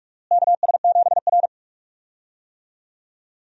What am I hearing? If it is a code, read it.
Morse code, heard as a keyed tone, about 43 wpm, 700 Hz. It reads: KS7R
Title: USA Call Signs - 4 Characters - Repeated 3x - 45wpm